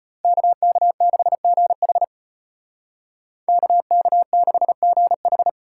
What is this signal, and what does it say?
Morse code, heard as a keyed tone, about 30 wpm, 700 Hz. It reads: KK6GH KK6GH
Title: Speed Racing - USA 2x2 Call Signs 25wpm